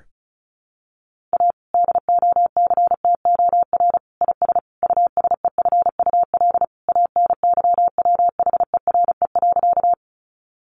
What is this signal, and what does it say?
Morse code, heard as a keyed tone, about 34 wpm, 700 Hz. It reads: A DOCTOR IS USEFUL ANYWHERE.
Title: Sentences from Top 1500 Words 35wpm